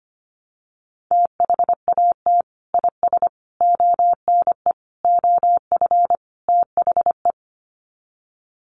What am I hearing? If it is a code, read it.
Morse code, heard as a keyed tone, about 25 wpm, 700 Hz. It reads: THAT IS ONE OF THE
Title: Sets of 5 Words Encore 25wpm